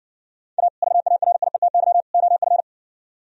Morse code, heard as a keyed tone, about 60 wpm, 700 Hz. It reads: A VARIETY OF